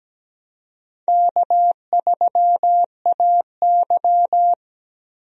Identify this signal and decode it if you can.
Morse code, heard as a keyed tone, about 17 wpm, 700 Hz. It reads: K3AY
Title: USA Call Signs - 4 Characters - Repeated 2x - 17wpm